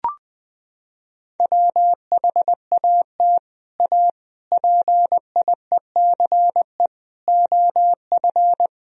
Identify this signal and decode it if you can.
Morse code, heard as a keyed tone, about 20 wpm, 700 Hz. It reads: WHAT A PIECE OF
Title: Sentences from Top 1000 Words - Review Part 2 - 20wpm